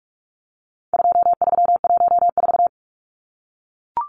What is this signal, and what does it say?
Morse code, heard as a keyed tone, about 44 wpm, 700 Hz. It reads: 2314
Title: Numbers - 4-Digit 45wpm